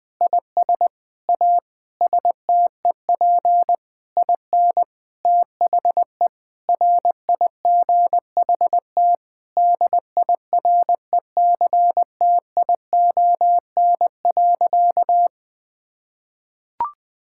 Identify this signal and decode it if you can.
Morse code, heard as a keyed tone, about 20 wpm, 700 Hz. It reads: IS A STEP IN THE RIGHT DIRECTION.
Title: Sentences with 6-Word Phrases - 20wpm